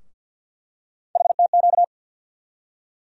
Morse code, heard as a keyed tone, about 47 wpm, 700 Hz. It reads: HTQ
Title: USA Call Sign Postfixes - 3 Letters 50wpm